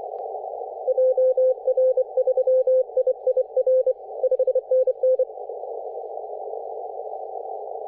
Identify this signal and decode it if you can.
Morse code, heard as a keyed tone, about 25 wpm, 520 Hz. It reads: JR3IIR 5NN
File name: 3D2YA 24.913MHz CW